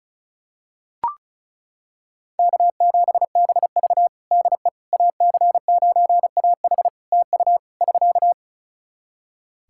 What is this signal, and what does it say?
Morse code, heard as a keyed tone, about 36 wpm, 700 Hz. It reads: K7BV DE AC9AH TU <SK>
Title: Practice QSO - Exchange 4 - Ending - 35wpm